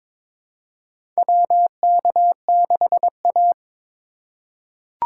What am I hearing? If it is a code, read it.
Morse code, heard as a keyed tone, about 21 wpm, 700 Hz. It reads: WK6A
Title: USA Call Signs - 4 Characters - 2x Character Spacing - 22wpm